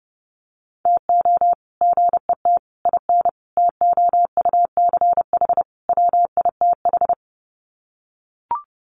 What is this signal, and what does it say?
Morse code, heard as a keyed tone, about 28 wpm, 700 Hz. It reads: TO GET IN TOUCH WITH
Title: Sets of 5 Words Encore2 30wpm